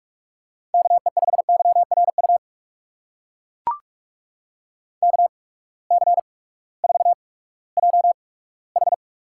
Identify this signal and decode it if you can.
Morse code, heard as a keyed tone, about 42 wpm, spaced as for 20 wpm, 700 Hz. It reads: KE5YRU KC4JH
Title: USA Call Signs - 6 Characters - 8x Character Spacing - 45wpm